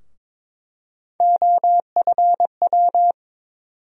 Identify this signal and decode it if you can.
Morse code, heard as a keyed tone, about 22 wpm, 700 Hz. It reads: OFW
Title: USA Call Sign Postfixes - 3 Letters 22wpm